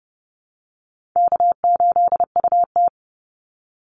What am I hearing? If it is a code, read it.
Morse code, heard as a keyed tone, about 30 wpm, 700 Hz. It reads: K8UT
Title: Easy Call Signs Repeated 3x - 30wpm